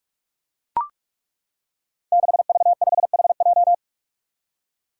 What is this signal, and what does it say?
Morse code, heard as a keyed tone, about 41 wpm, 700 Hz. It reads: 6V5HJ
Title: Five Challenging Letter-Number Combinations 45wpm